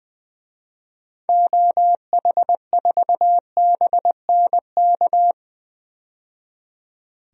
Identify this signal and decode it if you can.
Morse code, heard as a keyed tone, about 21 wpm, 700 Hz. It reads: OH4BNK
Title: DXCC-100 Call Signs - 6 Characters - 20wpm